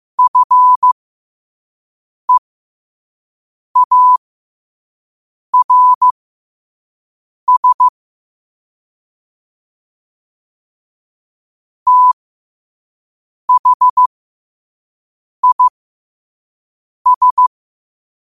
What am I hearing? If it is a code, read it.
Morse code, heard as a keyed tone, about 15 wpm, spaced as for 5 wpm, 1000 Hz. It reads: FEARS THIS
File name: qotdfiles / 5wpm